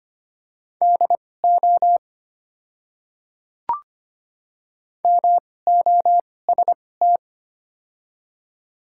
Morse code, heard as a keyed tone, about 25 wpm, spaced as for 18 wpm, 700 Hz. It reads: DO MOST
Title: 2x Character Spacing - Top 200 Words - 25wpm